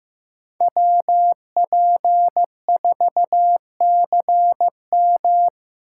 Morse code, heard as a keyed tone, about 15 wpm, 700 Hz. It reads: WP4CM
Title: USA Call Signs - 4x Character Spacing - 15wpm